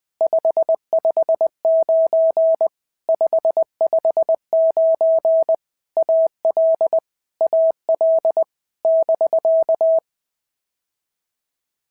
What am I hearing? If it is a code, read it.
Morse code, heard as a keyed tone, about 20 wpm, 650 Hz. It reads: 559 559 AL AL <BK>
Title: POTA QSO Practice - 20wpm